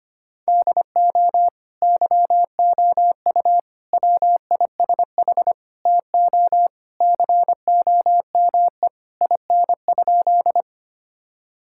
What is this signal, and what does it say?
Morse code, heard as a keyed tone, about 25 wpm, 700 Hz. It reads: DO YOU WISH TO COME IN?
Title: Sentences from Top 1600 Words 25wpm